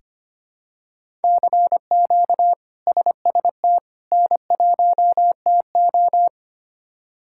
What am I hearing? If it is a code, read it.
Morse code, heard as a keyed tone, about 25 wpm, 700 Hz. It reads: CQ SST N1TO
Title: K1USN SST QSO Practice - 25wpm